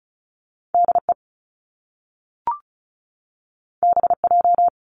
Morse code, heard as a keyed tone, about 36 wpm, 700 Hz. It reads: DE BJ
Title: Top Two Letter Combinations in Words 35wpm